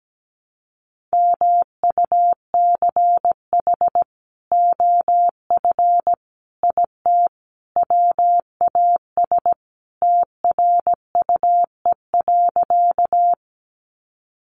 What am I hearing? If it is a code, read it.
Morse code, heard as a keyed tone, about 17 wpm, 700 Hz. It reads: MUCH OF IT WAS TRUE.
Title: Sentences from Top 400 Words 17wpm